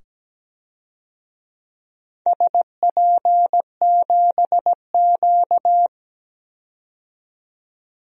Morse code, heard as a keyed tone, about 17 wpm, 700 Hz. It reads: SP7Q
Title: DXCC-100 Call Signs - 4 Characters - 17wpm